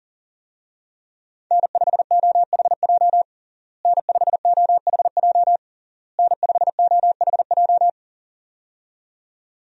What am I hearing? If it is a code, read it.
Morse code, heard as a keyed tone, about 40 wpm, 700 Hz. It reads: N5OHJ N5OHJ N5OHJ
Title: USA Call Signs - 5 Characters - Repeated 3x - 40wpm